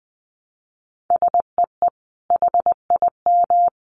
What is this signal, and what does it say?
Morse code, heard as a keyed tone, about 20 wpm, 700 Hz. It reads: SEE HIM
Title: Sentences from Top 300 Words 20wpm